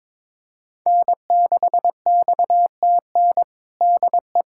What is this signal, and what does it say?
Morse code, heard as a keyed tone, about 21 wpm, 700 Hz. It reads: N6XTN DE
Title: USA Callsign DE Callsign 22wpm